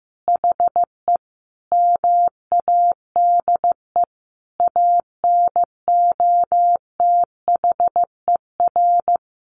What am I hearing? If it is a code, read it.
Morse code, heard as a keyed tone, about 15 wpm, 700 Hz. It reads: HE MADE ANOTHER
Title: Sentences from Top 1000 Words 15wpm